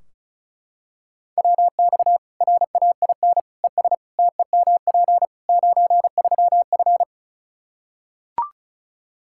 Morse code, heard as a keyed tone, about 36 wpm, 700 Hz. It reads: WX RAIN ES TEMP 93F
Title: Practice QSO - Exchange 2 - Info - 35wpm